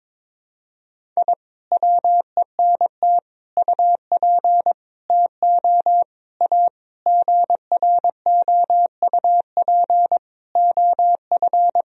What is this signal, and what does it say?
Morse code, heard as a keyed tone, about 22 wpm, 700 Hz. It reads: I WENT UP TO A GROUP OF
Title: Sentences from Top 1000 Words - Review Part 1 - 22wpm